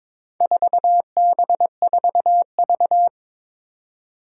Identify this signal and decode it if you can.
Morse code, heard as a keyed tone, about 22 wpm, 700 Hz. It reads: 4B4V
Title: Four Challenging Letter-Number Combinations 22wpm